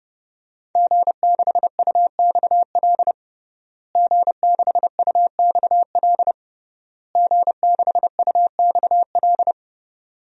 Morse code, heard as a keyed tone, about 30 wpm, 700 Hz. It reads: G6UXL G6UXL G6UXL
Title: DXCC-100 Call Signs - 5 Characters - Repeated 3x - 30wpm